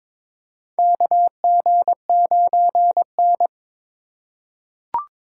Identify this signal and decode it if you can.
Morse code, heard as a keyed tone, about 21 wpm, 700 Hz. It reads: KG9N
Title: USA Call Signs - 4 Characters - 4x Character Spacing - 22wpm